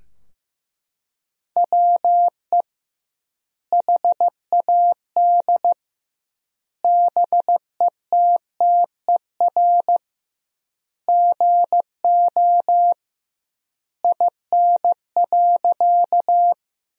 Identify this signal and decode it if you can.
Morse code, heard as a keyed tone, about 15 wpm, 700 Hz. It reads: WE HAD BETTER GO IN.
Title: 2x Word-Spacing - Sentences from Top 400 Words - 15wpm